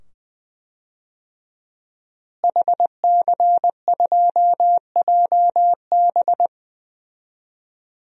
Morse code, heard as a keyed tone, about 20 wpm, 700 Hz. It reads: HC2JB
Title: DXCC-100 Call Signs - 5 Characters - Repeated 2x - 20wpm